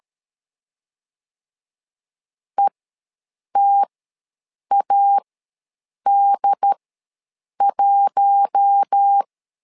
Morse code, heard as a keyed tone, about 13 wpm, spaced as for 7 wpm, 800 Hz. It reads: ETAD1
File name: LEZIONI / 4 Q1 ws 12wpm 2129